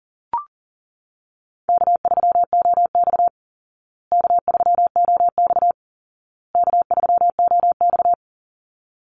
Morse code, heard as a keyed tone, about 37 wpm, 700 Hz. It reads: K3OX K3OX K3OX
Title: Easy Call Signs Repeated 3x - 40wpm